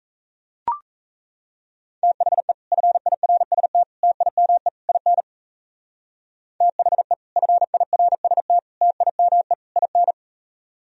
Morse code, heard as a keyed tone, about 34 wpm, 700 Hz. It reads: THE FIRST TIME IN THE FIRST TIME IN
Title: Speed-Racing - Sets of 4 Words 28wpm